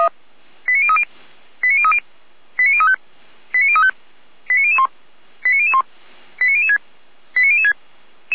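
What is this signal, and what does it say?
Five-tone selective call (ZVEI), heard as quick runs of five tones, tones of about 70 ms. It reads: 89030 89030 89035 89035 89001 89001 89006 89006
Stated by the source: ZVEI five-tone selective call 89508 received by SDR, from the Feuerwehr Orlamünde fire brigade server.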